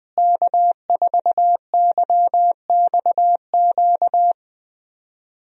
Morse code, heard as a keyed tone, about 20 wpm, 700 Hz. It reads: K4YXQ
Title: USA Call Signs - 5 Characters - Repeated 3x - 20wpm